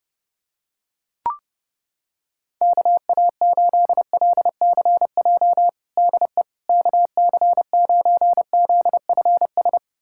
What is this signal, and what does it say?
Morse code, heard as a keyed tone, about 30 wpm, 700 Hz. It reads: KA8LCJ DE KC9ZFS
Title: USA Callsign DE Callsign - 6 Characters - 30wpm